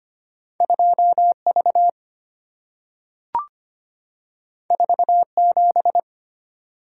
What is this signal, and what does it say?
Morse code, heard as a keyed tone, about 25 wpm, 690 Hz. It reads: 2V 47
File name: Two Letter-Number Combinations - 25wpm